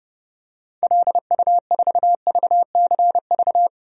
Morse code, heard as a keyed tone, about 30 wpm, 700 Hz. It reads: LU4VCV
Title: DXCC-100 Call Signs - 6 Characters - 30wpm